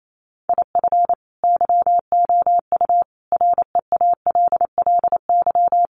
Morse code, heard as a keyed tone, about 28 wpm, 700 Hz. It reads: IF YOU REALLY
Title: Sets of 5 Words Encore2 28wpm